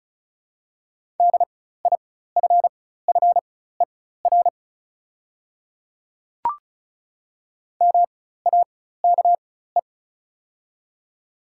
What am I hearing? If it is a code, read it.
Morse code, heard as a keyed tone, about 36 wpm, spaced as for 16 wpm, 690 Hz. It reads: DIFFER MAKE
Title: 4x Character Spacing - Top 200 Words - 35wpm